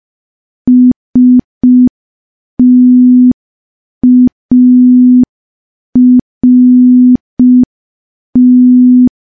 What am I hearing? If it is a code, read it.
Morse code, heard as a keyed tone, about 5 wpm, 250 Hz. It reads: START
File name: morse / morse-training-05WPM